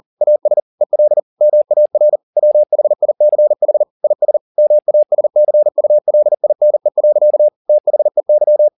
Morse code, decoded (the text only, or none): AS EL MAR WHICH IS MASCULINE. THEY